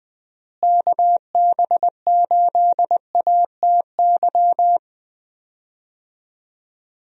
Morse code, heard as a keyed tone, about 20 wpm, 700 Hz. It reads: KB8ATY
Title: USA Call Signs - 6 Characters - Repeated 2x - 20wpm